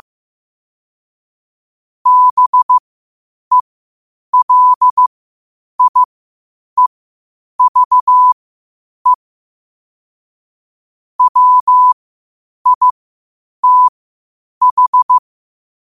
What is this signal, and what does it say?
Morse code, heard as a keyed tone, about 15 wpm, spaced as for 8 wpm, 1000 Hz. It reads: BELIEVE WITH
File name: qotdfiles / 8wpm